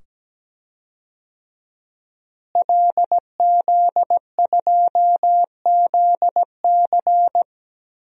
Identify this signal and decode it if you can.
Morse code, heard as a keyed tone, about 17 wpm, 700 Hz. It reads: LZ2ZC
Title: DXCC-100 Call Signs - 5 Characters - Repeated 2x - 17wpm